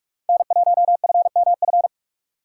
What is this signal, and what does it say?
Morse code, heard as a keyed tone, about 58 wpm, 700 Hz. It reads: N1FMF